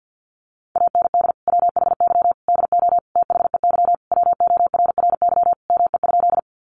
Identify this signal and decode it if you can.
Morse code, heard as a keyed tone, about 50 wpm, 700 Hz. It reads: AND WHY DO THEY WORRY ME?
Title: Sentences from Top 1100 Words 50wpm